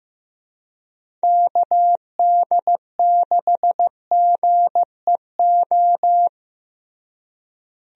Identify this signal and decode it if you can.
Morse code, heard as a keyed tone, about 15 wpm, 700 Hz. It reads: KD6GEO
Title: USA Call Signs - 6 Characters - Repeated 2x - 15wpm